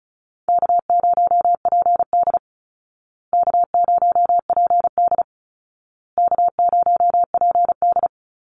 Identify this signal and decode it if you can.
Morse code, heard as a keyed tone, about 34 wpm, 700 Hz. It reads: K0PD K0PD K0PD
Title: Easy Call Signs Repeated 3x - 35wpm